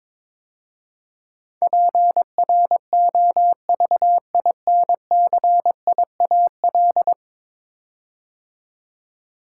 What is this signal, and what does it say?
Morse code, heard as a keyed tone, about 23 wpm, 700 Hz. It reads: PROVINCIAL
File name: practice / 22wpm / 10 Letter Words - First 3 Letters Given - 22wpm